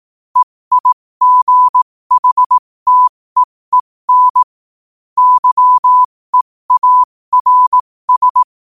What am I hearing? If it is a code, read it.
Morse code, heard as a keyed tone, about 18 wpm, spaced as for 15 wpm, 1000 Hz. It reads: EIGHTEEN YEARS
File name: 15wpm